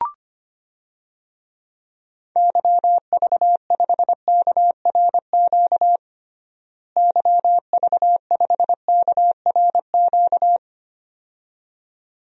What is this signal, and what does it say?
Morse code, heard as a keyed tone, about 25 wpm, 700 Hz. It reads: YV5KRQ YV5KRQ
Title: DXCC-100 Call Signs - 6 Characters - Repeated 2x - 25wpm